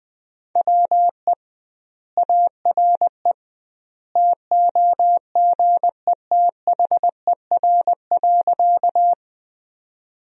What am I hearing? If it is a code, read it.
Morse code, heard as a keyed tone, about 20 wpm, 700 Hz. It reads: WE ARE TOGETHER.
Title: 2x Word-Spacing - Sentences from Top 300 Words - 20wpm